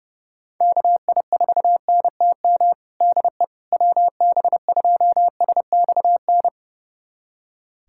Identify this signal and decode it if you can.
Morse code, heard as a keyed tone, about 30 wpm, 700 Hz. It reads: KI4NTM DE WB2SXN
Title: USA Callsign DE Callsign 30wpm